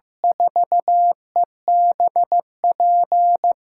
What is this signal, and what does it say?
Morse code, heard as a keyed tone, about 15 wpm, 700 Hz. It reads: 4EBP
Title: USA Call Signs - 6 Characters - 8x Character Spacing - 15wpm